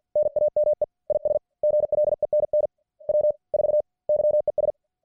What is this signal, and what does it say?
Morse code, heard as a keyed tone, about 48 wpm, 600 Hz. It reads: NAME IS GLENN W 4 YES